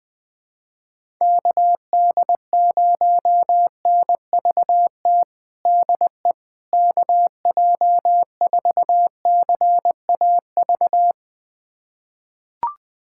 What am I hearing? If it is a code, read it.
Morse code, heard as a keyed tone, about 20 wpm, 700 Hz. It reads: KD0NVT DE KJ4CAV
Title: USA Callsign DE Callsign 20wpm